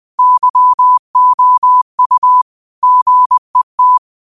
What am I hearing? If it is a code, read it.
Morse code, heard as a keyed tone, about 20 wpm, 1000 Hz. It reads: YOU GET
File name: qotdfiles / 20wpm